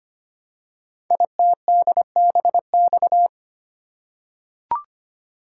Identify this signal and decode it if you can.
Morse code, heard as a keyed tone, about 25 wpm, 700 Hz. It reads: ITDBX
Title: Code Groups - 5 Letters - 25wpm